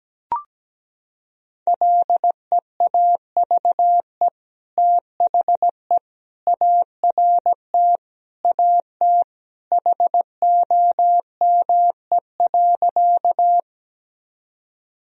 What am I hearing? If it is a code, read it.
Morse code, heard as a keyed tone, about 17 wpm, 700 Hz. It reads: LEAVE THE ART AT HOME.
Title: Sentences from Top 1000 Words - Review Part 3 - 17wpm